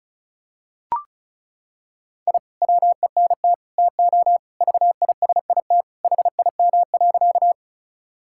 Morse code, heard as a keyed tone, about 36 wpm, 700 Hz. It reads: I WENT TO VISIT HIM.
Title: Sentences from Top 1000 Words - Review Part 6 - 35wpm